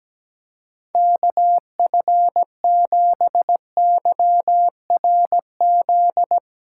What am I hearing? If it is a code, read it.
Morse code, heard as a keyed tone, about 17 wpm, 700 Hz. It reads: KF7YRZ
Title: USA Callsign DE Callsign - 6 Characters - 17wpm